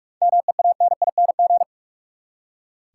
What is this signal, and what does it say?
Morse code, heard as a keyed tone, about 43 wpm, 700 Hz. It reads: MEANING
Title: Seven Letter Words - 45wpm